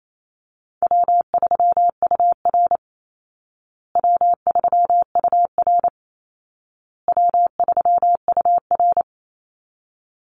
Morse code, heard as a keyed tone, about 28 wpm, 700 Hz. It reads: W3UR W3UR W3UR
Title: Easy Call Signs Repeated 3x - 28wpm